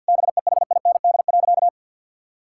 Morse code, heard as a keyed tone, about 50 wpm, 700 Hz. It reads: BEHIND.
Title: Sentences from Top 1600 Words 50wpm